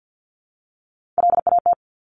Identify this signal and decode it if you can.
Morse code, heard as a keyed tone, about 50 wpm, 700 Hz. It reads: LAT
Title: Top Three Letter Combinations in Words 50wpm